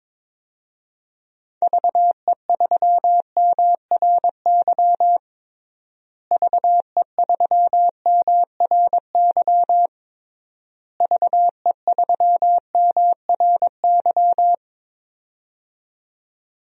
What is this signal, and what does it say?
Morse code, heard as a keyed tone, about 21 wpm, 700 Hz. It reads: VE3MRY VE3MRY VE3MRY
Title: DXCC-100 Call Signs - 6 Characters - Repeated 3x - 22wpm